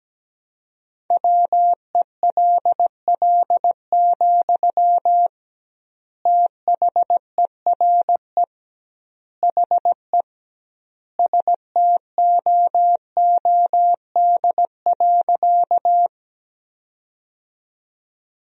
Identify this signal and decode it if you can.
Morse code, heard as a keyed tone, about 17 wpm, 700 Hz. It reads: WELL, THERE HE STOOD.
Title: 2x Word-Spacing - Sentences from Top 500 Words - 17wpm